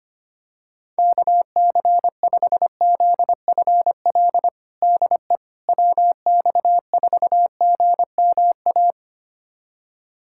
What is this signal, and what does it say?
Morse code, heard as a keyed tone, about 25 wpm, 700 Hz. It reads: KC5ZFL DE WX4GMA
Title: USA Callsign DE Callsign - 6 Characters - 25wpm